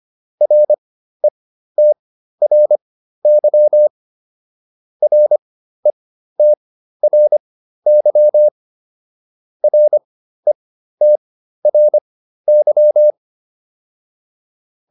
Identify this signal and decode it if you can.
Morse code, heard as a keyed tone, about 25 wpm, spaced as for 13 wpm, 600 Hz. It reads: RETRY RETRY RETRY